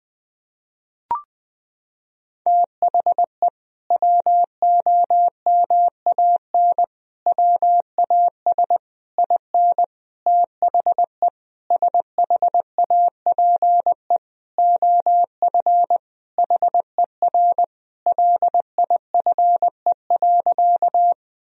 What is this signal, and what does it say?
Morse code, heard as a keyed tone, about 21 wpm, 700 Hz. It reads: THE WOMAN WAS IN THE SHAPE OF HER LIFE.
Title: Sentences from Top 1000 Words - Review Part 3 - 20wpm